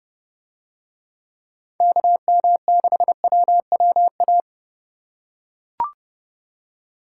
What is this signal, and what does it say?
Morse code, heard as a keyed tone, about 30 wpm, 700 Hz. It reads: KM6WWA